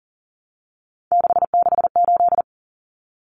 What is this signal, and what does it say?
Morse code, heard as a keyed tone, about 40 wpm, 700 Hz. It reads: 668
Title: Common Cut Numbers - 3-Digit 40wpm